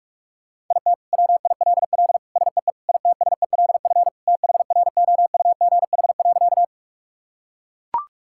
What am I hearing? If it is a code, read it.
Morse code, heard as a keyed tone, about 42 wpm, 700 Hz. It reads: IT WILL SEE ITSELF THROUGH.
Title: Sentences from Top 1000 Words - Review Part 5 - 45wpm